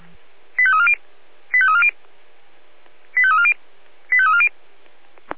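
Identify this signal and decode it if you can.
Five-tone selective call (ZVEI), heard as quick runs of five tones, tones of about 70 ms. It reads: 85340 85340 85340 85340
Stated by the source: ZVEI five-tone call received on an SDR.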